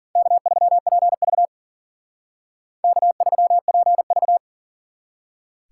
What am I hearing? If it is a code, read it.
Morse code, heard as a keyed tone, about 51 wpm, 700 Hz. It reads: K3PV K3PV